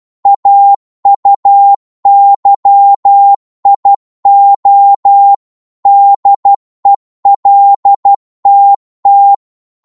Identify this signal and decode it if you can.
Morse code, heard as a keyed tone, about 12 wpm, 800 Hz. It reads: AUYIO DELTT
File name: Morse / ltr-12wpm